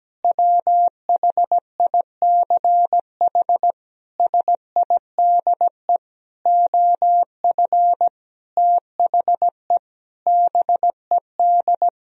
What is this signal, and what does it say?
Morse code, heard as a keyed tone, about 17 wpm, 700 Hz. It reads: WHICH SIDE OF THE BED
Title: Sentences from Top 1000 Words - Review Part 1 - 17wpm